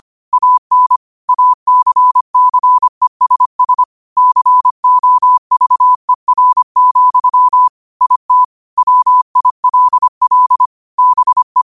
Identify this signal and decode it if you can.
Morse code, heard as a keyed tone, about 25 wpm, 1000 Hz. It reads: AN ACCESS COVER, IT WILL BE